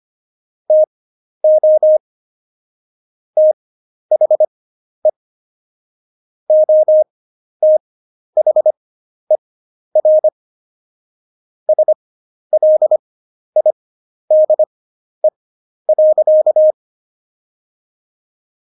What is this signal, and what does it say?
Morse code, heard as a keyed tone, about 25 wpm, spaced as for 11 wpm, 620 Hz. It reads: TO THE OTHER SLIDE.
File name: Jokes-25-char-10-eff